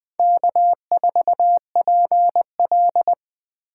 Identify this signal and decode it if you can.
Morse code, heard as a keyed tone, about 19 wpm, 700 Hz. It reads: K4PL